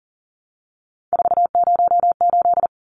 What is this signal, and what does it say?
Morse code, heard as a keyed tone, about 40 wpm, 700 Hz. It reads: 408
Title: Numbers - 3-Digit 40wpm